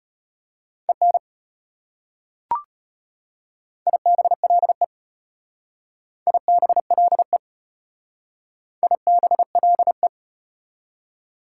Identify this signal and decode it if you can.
Morse code, heard as a keyed tone, about 34 wpm, 700 Hz. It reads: EN IBLE IBLE IBLE